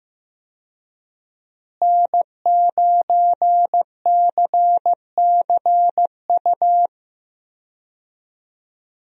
Morse code, heard as a keyed tone, about 15 wpm, 700 Hz. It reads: N9CCU
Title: USA Call Signs - 5 Characters - Repeated 3x - 15wpm